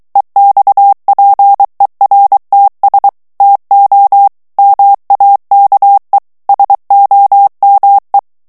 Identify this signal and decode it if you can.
Morse code, heard as a keyed tone, about 23 wpm, 800 Hz. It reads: EXPERTS TO MAKE SOME
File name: morse mp3 / ANGLAIS texte clair 2 vitesse 22 mots minute